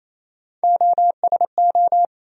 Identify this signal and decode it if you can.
Morse code, heard as a keyed tone, about 28 wpm, 700 Hz. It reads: OSO
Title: USA Call Sign Postfixes - 3 Letters 28wpm